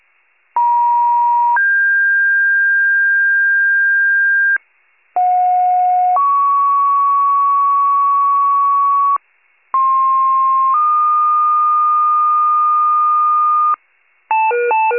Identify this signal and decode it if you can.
Two-tone page, heard as a one-second tone, then a three-second tone, about 950 then 1600 Hz, about 730 then 1120 Hz, about 1060 then 1220 Hz. It is heard from 0.6 to 13.7 s.
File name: pager / 98 RESCUE / wcalert